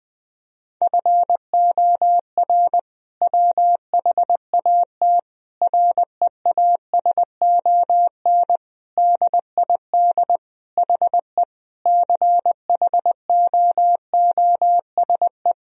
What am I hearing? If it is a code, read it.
Morse code, heard as a keyed tone, about 20 wpm, 700 Hz. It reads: FOR WHAT REASON DID HE CHOOSE
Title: Sentences from Top 1000 Words - Review Part 4 - 20wpm